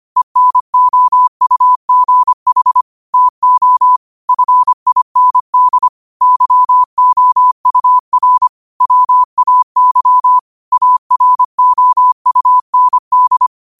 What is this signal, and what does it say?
Morse code, heard as a keyed tone, about 25 wpm, 1000 Hz. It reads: ENOUGH TO FIND YOUR WAY AROUND
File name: qotdfiles / 25wpm